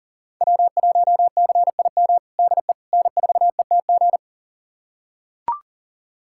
Morse code, heard as a keyed tone, about 37 wpm, 700 Hz. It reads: W1CIM DE N4ETG
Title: USA Callsign DE Callsign - 5 Characters - 40wpm